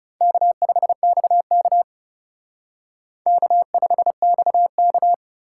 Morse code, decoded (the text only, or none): K5XK K5XK